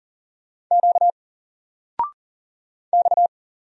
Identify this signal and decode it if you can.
Morse code, heard as a keyed tone, about 40 wpm, 700 Hz. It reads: Q X
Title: Single Letter-Number 40wpm